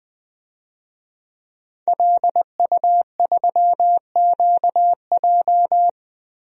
Morse code, heard as a keyed tone, about 20 wpm, 700 Hz. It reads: LU3QJ
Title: DXCC-100 Call Signs - 5 Characters - 20wpm